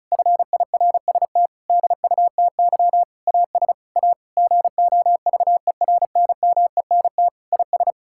Morse code, heard as a keyed tone, about 34 wpm, 700 Hz. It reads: FIRST DUTY AS A GOVERNMENT IS
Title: Sentences from Top 1600 Words 35wpm